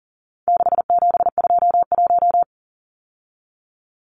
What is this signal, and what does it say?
Morse code, heard as a keyed tone, about 40 wpm, 700 Hz. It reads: 6721
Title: Numbers - 4-Digit 40wpm